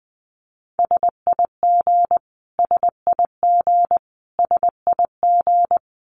Morse code, heard as a keyed tone, about 19 wpm, 700 Hz. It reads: SIG SIG SIG
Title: Abbreviations Repeated 3x - 20wpm